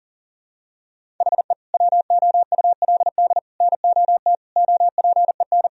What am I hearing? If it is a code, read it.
Morse code, heard as a keyed tone, about 40 wpm, 700 Hz. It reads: HE WOULD NOT OPEN
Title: Sentences from Top 1000 Words - Review Part 1 - 40wpm